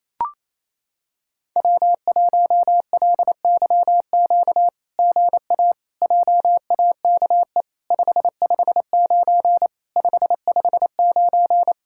Morse code, heard as a keyed tone, about 28 wpm, 700 Hz. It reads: W1LYQ GA JAKE 559 559
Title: SOTA QSO Practice 28wpm